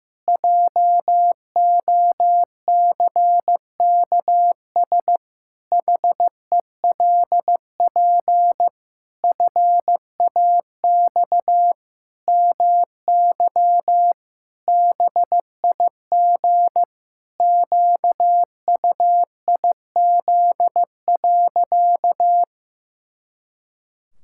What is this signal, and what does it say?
Morse code, heard as a keyed tone, about 15 wpm, 700 Hz. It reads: JOCKS HELP FAX MY BIG QUIZ.
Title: Pangram Challenge v2 - 15wpm